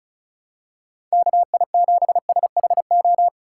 Morse code, heard as a keyed tone, about 36 wpm, 700 Hz. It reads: KI7SHO
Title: USA Call Signs - 6 Characters - 3x Character Spacing - 35wpm